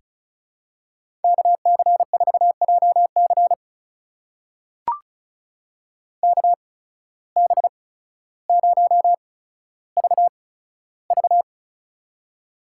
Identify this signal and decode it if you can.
Morse code, heard as a keyed tone, about 36 wpm, spaced as for 15 wpm, 700 Hz. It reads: KC4JC KB0VV
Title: USA Call Signs - 5 Characters - 8x Character Spacing - 35wpm